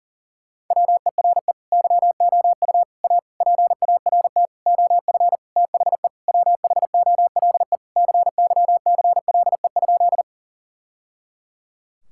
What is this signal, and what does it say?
Morse code, heard as a keyed tone, about 40 wpm, 700 Hz. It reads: WERE YOU A PART OF THE WHOLE CYCLE?
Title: Sentences from Top 1000 Words - Review Part 6 - 40wpm